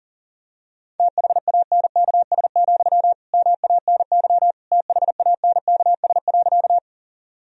Morse code, heard as a keyed tone, about 37 wpm, 700 Hz. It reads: THANKS, MANY THANKS.
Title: Sentences from Top 1000 Words - Review Part 3 - 40wpm